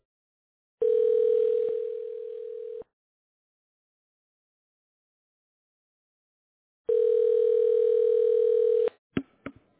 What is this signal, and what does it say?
Ringback tone (North America), 440 + 480 Hz.